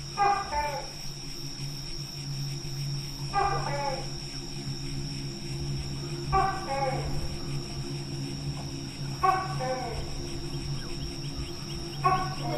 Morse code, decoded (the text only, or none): none